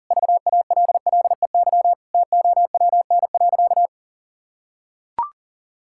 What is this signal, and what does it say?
Morse code, heard as a keyed tone, about 37 wpm, 700 Hz. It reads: VALLEY TOWN.
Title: Pangram Challenge v2 - 40wpm